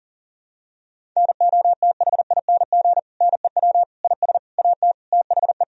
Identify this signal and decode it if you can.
Morse code, heard as a keyed tone, about 40 wpm, 700 Hz. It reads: NOTHING NEW IS AT THE
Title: Sentences from Top 1000 Words - Review Part 1 - 40wpm